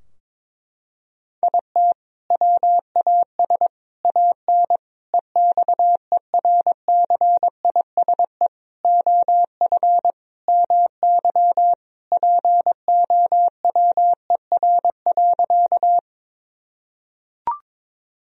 Morse code, heard as a keyed tone, about 22 wpm, 700 Hz. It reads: IT WAS AN EXERCISE OF MY POWER.
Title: Sentences from Top 1000 Words - Review Part 4 - 22wpm